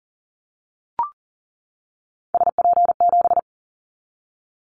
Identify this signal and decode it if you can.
Morse code, heard as a keyed tone, about 40 wpm, 700 Hz. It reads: SP7
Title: Callsign Prefixes for Top 25 DXCC - 40wpm